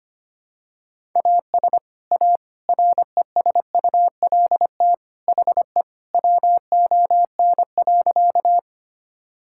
Morse code, heard as a keyed tone, about 25 wpm, 700 Hz. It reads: AS A RESULT HE WON.
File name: practice / 25wpm / Sentences with 3-Word Phrases - 25wpm